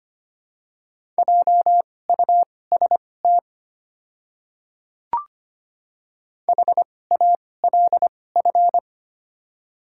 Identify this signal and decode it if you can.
Morse code, heard as a keyed tone, about 25 wpm, spaced as for 18 wpm, 700 Hz. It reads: JUST HALF
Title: 2x Character Spacing - Top 500 Words - 25wpm